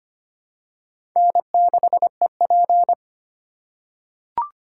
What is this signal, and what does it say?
Morse code, heard as a keyed tone, about 25 wpm, 700 Hz. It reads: N6EP
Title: USA Call Signs - 3x Character Spacing - 25wpm